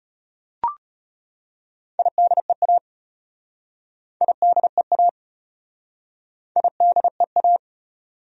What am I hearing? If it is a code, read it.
Morse code, heard as a keyed tone, about 33 wpm, 700 Hz. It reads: IDEA IDEA IDEA